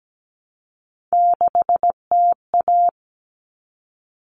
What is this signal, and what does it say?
Morse code, heard as a keyed tone, about 17 wpm, 700 Hz. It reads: 6TA
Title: All Cut Numbers - 3-Digit 17wpm